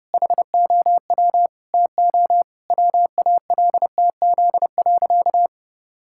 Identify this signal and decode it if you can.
Morse code, heard as a keyed tone, about 32 wpm, 700 Hz. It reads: HOW TO WALTZ.
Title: Pangram Challenge v2 - 30wpm